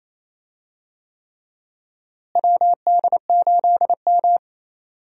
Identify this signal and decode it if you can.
Morse code, heard as a keyed tone, about 28 wpm, 700 Hz. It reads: WD8M
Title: USA Call Signs - 4 Characters - Repeated 3x - 28wpm